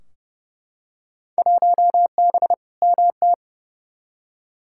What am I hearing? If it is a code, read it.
Morse code, heard as a keyed tone, about 30 wpm, 700 Hz. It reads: 1B MT
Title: Field Day Exchanges - 30wpm